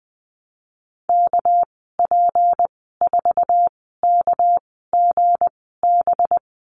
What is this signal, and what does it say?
Morse code, heard as a keyed tone, about 20 wpm, spaced as for 15 wpm, 700 Hz. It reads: KP4KGB
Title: 231 Lesson 37g - Letter x - Call Signs 20wpm